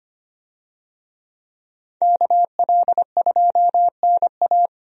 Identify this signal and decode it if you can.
Morse code, heard as a keyed tone, about 25 wpm, 700 Hz. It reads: KL2NA
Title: USA Call Signs - 5 Characters - 25wpm